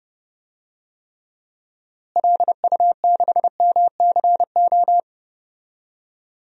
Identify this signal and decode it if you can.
Morse code, heard as a keyed tone, about 30 wpm, 700 Hz. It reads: LU6MCO
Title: DXCC-100 Call Signs - 6 Characters - Repeated 2x - 30wpm